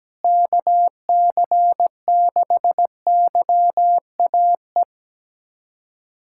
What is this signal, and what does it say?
Morse code, heard as a keyed tone, about 17 wpm, 700 Hz. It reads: KC6YAE
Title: USA Callsign DE Callsign 17wpm